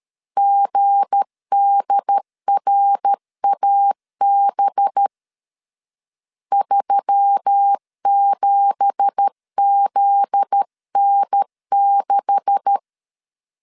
Morse code, heard as a keyed tone, about 13 wpm, 800 Hz. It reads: GDRAB 37ZN6
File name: LEZIONI / 9 2867 12wpm 2322